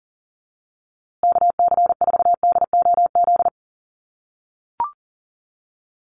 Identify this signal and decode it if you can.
Morse code, heard as a keyed tone, about 40 wpm, 700 Hz. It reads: KC4DOZ
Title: USA Call Signs Repeated 3x 40wpm